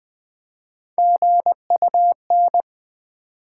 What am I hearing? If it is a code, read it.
Morse code, heard as a keyed tone, about 20 wpm, 700 Hz. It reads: GUN
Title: Build A Word - 1 Letter At A Time - 3-Letter Words - 20wpm